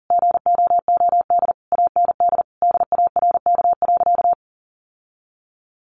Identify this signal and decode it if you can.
Morse code, heard as a keyed tone, about 40 wpm, 700 Hz. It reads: GOOD AND DARK.
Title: Sentences from Top 600 Words 40wpm